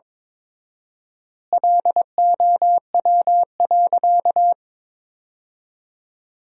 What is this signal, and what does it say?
Morse code, heard as a keyed tone, about 21 wpm, 700 Hz. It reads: LOW.
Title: 4x Word-Spacing - Sentences from Top 500 Words - 22wpm